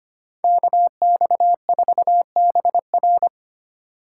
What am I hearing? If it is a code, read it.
Morse code, heard as a keyed tone, about 33 wpm, 700 Hz. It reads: KX4BR